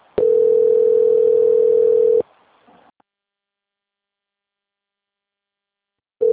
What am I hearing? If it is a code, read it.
Ringback tone (North America), 440 + 480 Hz.